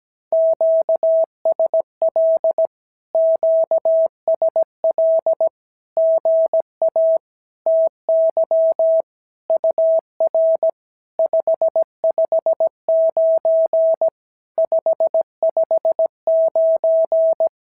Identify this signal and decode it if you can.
Morse code, heard as a keyed tone, about 17 wpm, 650 Hz. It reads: QSL QSL GA TY UR 559 559
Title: SOTA QSO Practice 17wpm